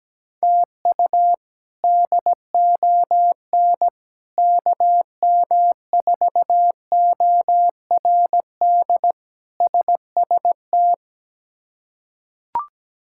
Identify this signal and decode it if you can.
Morse code, heard as a keyed tone, about 17 wpm, 700 Hz. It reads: TU DON KM4ORD SST
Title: K1USN SST QSO Practice - 17wpm